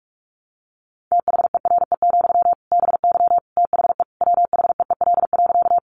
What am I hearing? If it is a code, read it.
Morse code, heard as a keyed tone, about 44 wpm, 700 Hz. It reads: THERE, BY THE WHEEL.
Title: Sentences from Top 500 Words 45wpm